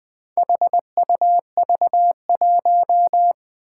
Morse code, heard as a keyed tone, about 20 wpm, 700 Hz. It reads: HUV1
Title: Four Challenging Letter-Number Combinations 20wpm